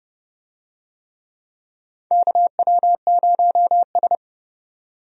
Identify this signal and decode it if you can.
Morse code, heard as a keyed tone, about 30 wpm, 700 Hz. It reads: KW0S
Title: USA Call Signs - 4 Characters - 30wpm